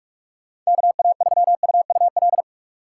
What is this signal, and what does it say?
Morse code, heard as a keyed tone, about 48 wpm, 700 Hz. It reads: KA3UUL
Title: USA Call Signs - 6 Characters - 8x Character Spacing - 45wpm